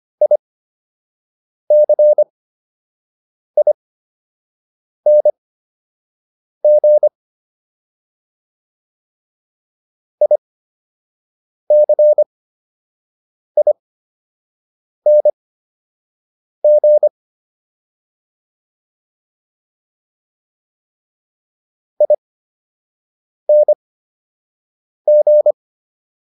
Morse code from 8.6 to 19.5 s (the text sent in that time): ICING